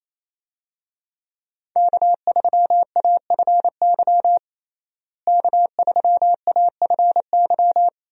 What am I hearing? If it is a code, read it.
Morse code, heard as a keyed tone, about 28 wpm, 700 Hz. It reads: K3AFY K3AFY
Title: USA Call Signs - 5 Characters - Repeated 2x - 28wpm